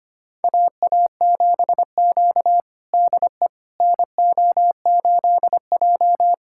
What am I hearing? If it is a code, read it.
Morse code, heard as a keyed tone, about 25 wpm, 700 Hz. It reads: AA7Q DE NO8J
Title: USA Callsign DE Callsign - 4 Characters - 25wpm